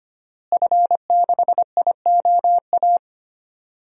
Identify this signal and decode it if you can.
Morse code, heard as a keyed tone, about 25 wpm, 700 Hz. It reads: F6IOA
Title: DXCC-100 Call Signs - 5 Characters - Repeated 3x - 25wpm